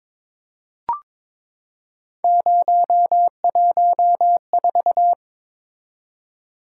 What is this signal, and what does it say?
Morse code, heard as a keyed tone, about 22 wpm, 700 Hz. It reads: 014
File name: practice / 22wpm / Three Letter-Number Combinations - 22wpm